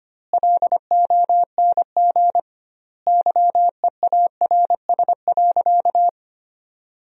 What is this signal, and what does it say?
Morse code, heard as a keyed tone, about 25 wpm, 700 Hz. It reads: LONG YEARS.
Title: 2x Word-Spacing - Sentences from Top 200 Words - 25wpm